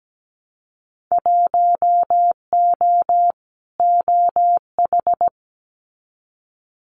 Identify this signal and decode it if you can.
Morse code, heard as a keyed tone, about 17 wpm, 700 Hz. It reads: 1O OH
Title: Winter Field Day Exchanges 17wpm